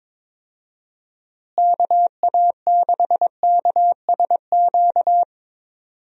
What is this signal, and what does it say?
Morse code, heard as a keyed tone, about 22 wpm, 700 Hz. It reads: KA6KSQ
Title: USA Call Signs - 6 Characters - Repeated 3x - 22wpm